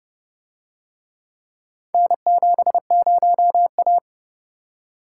Morse code, heard as a keyed tone, about 29 wpm, 700 Hz. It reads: N70A